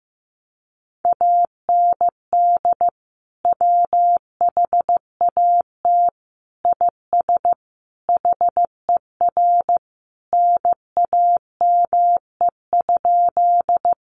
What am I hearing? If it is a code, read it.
Morse code, heard as a keyed tone, about 15 wpm, 700 Hz. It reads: AND WHAT IS HER NAME?
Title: Sentences from Top 200 Words 15wpm